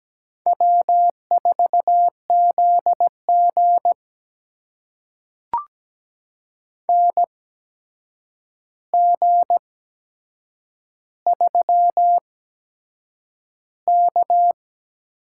Morse code, heard as a keyed tone, about 17 wpm, spaced as for 7 wpm, 700 Hz. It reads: W4ZG NG3K
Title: USA Call Signs - 4 Characters - 8x Character Spacing - 17wpm